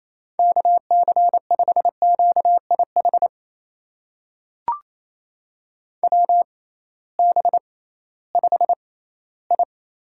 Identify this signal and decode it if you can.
Morse code, heard as a keyed tone, about 28 wpm, spaced as for 16 wpm, 700 Hz. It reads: KC5QIH WB5I
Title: USA Call Signs - 6 Characters - 6x Character Spacing - 28wpm